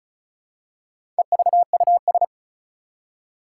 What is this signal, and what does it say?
Morse code, heard as a keyed tone, about 36 wpm, 700 Hz. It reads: EVUS
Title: Four Challenging Letter-Number Combinations 35wpm